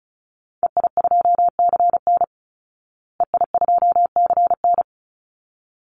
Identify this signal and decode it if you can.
Morse code, heard as a keyed tone, about 34 wpm, 700 Hz. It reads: EI2CN EI2CN
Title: Call Signs Repeated 3x - 35wpm